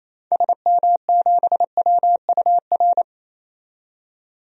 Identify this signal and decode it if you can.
Morse code, heard as a keyed tone, about 28 wpm, 700 Hz. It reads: SM7WUR
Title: DXCC-100 Call Signs - 6 Characters - 28wpm